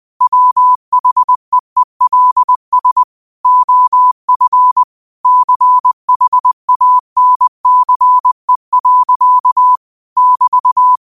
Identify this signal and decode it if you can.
Morse code, heard as a keyed tone, about 20 wpm, 1000 Hz. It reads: WHEELS OF CHANCE. <BT>
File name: qotdfiles / 20wpm